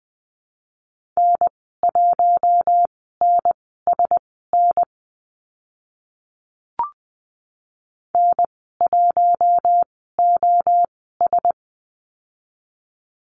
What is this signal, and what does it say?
Morse code, heard as a keyed tone, about 20 wpm, spaced as for 14 wpm, 700 Hz. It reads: N1NSN N1OS
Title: 042 Lesson 8g - Number 1 - Call Signs 20wpm